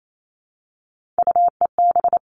Morse code, heard as a keyed tone, about 28 wpm, 700 Hz. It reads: UEB
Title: All Cut Numbers - 3-Digit 28wpm